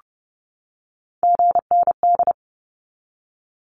Morse code, heard as a keyed tone, about 30 wpm, 700 Hz. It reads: GND
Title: Abbreviations 30wpm